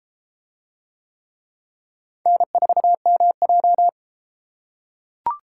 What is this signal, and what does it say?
Morse code, heard as a keyed tone, about 33 wpm, 700 Hz. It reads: N4MJ